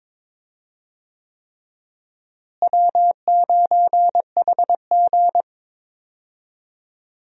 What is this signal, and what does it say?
Morse code, heard as a keyed tone, about 22 wpm, 700 Hz. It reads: W9HG
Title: USA Call Signs - 4 Characters - Repeated 3x - 22wpm